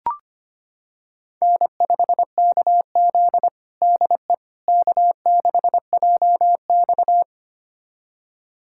Morse code, heard as a keyed tone, about 25 wpm, 700 Hz. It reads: N5KZ DE K6JX
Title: USA Callsign DE Callsign - 4 Characters - 25wpm